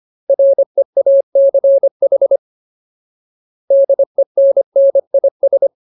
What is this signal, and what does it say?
Morse code, heard as a keyed tone, about 25 wpm, 550 Hz. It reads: REACH DENNIS
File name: Ultimate Word List / 25WPM Ultimate